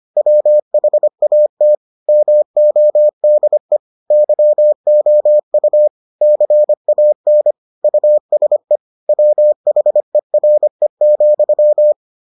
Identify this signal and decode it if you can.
Morse code, heard as a keyed tone, about 25 wpm, 600 Hz. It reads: WHAT MODE YOU CAN USE WHERE,